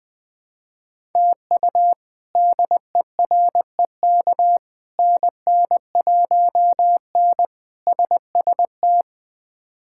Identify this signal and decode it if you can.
Morse code, heard as a keyed tone, about 21 wpm, 700 Hz. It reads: TU DEREK NN1N SST
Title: K1USN SST QSO Practice - 20wpm